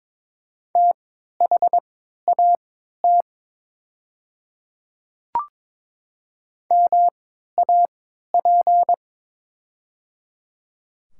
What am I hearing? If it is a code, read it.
Morse code, heard as a keyed tone, about 22 wpm, spaced as for 12 wpm, 700 Hz. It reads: THAT MAP
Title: 3x Character Spacing - Top 500 Words - 22wpm